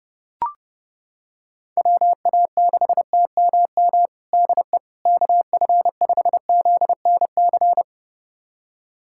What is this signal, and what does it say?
Morse code, heard as a keyed tone, about 32 wpm, 700 Hz. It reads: WA6TMM DE KF5ZNC
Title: USA Callsign DE Callsign - 6 Characters - 30wpm